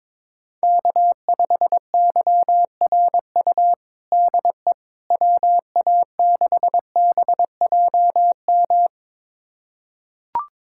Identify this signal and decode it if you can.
Morse code, heard as a keyed tone, about 22 wpm, 700 Hz. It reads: K5YRU DE WA6BJM
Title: USA Callsign DE Callsign 22wpm